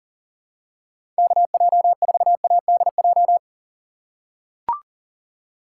Morse code, heard as a keyed tone, about 40 wpm, 700 Hz. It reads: KJ4ADJ
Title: USA Call Signs - 6 Characters - 6x Character Spacing - 40wpm